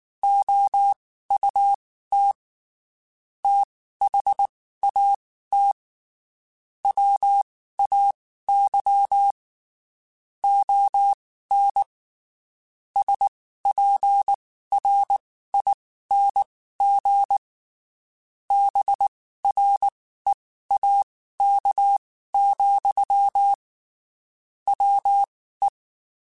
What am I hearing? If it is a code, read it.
Morse code, heard as a keyed tone, about 19 wpm, spaced as for 14 wpm, 800 Hz. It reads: OUT THAT WAY ON SPRING BREAK, WE
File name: (spaced) / QSOs Spaced-mp3s / WA7QOX 1 @18 sp